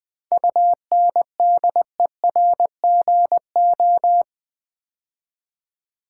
Morse code, heard as a keyed tone, about 20 wpm, 700 Hz. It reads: UNDERGO
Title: 7 Letter Words - First 3 Letters Given - 20wpm